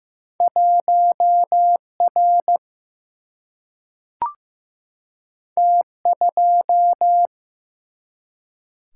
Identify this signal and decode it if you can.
Morse code, heard as a keyed tone, about 15 wpm, 700 Hz. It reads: 1R T2
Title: Single-Letter Single-Number 15wpm